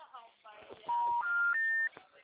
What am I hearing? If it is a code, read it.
Special information tone, with speech.
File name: out- -2012- - - .26